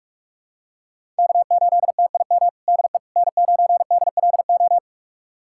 Morse code, heard as a keyed tone, about 41 wpm, 700 Hz. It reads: K8TIM DE N9DLO
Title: USA Callsign DE Callsign - 5 Characters - 45wpm